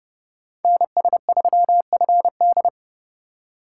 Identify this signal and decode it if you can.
Morse code, heard as a keyed tone, about 28 wpm, 700 Hz. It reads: NS3FD